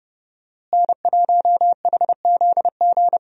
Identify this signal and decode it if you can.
Morse code, heard as a keyed tone, about 30 wpm, 700 Hz. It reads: N1HZG
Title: USA Call Signs - 5 Characters - Repeated 3x - 30wpm